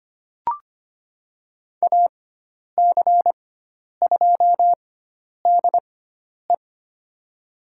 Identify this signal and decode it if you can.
Morse code, heard as a keyed tone, about 25 wpm, spaced as for 10 wpm, 700 Hz. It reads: AC2DE
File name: Lesson 16g - Letter c - Call Signs 25wpm